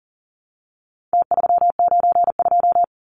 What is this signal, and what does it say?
Morse code, heard as a keyed tone, about 40 wpm, 700 Hz. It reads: T392
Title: Numbers - Contest Style - 4-Digit 40wpm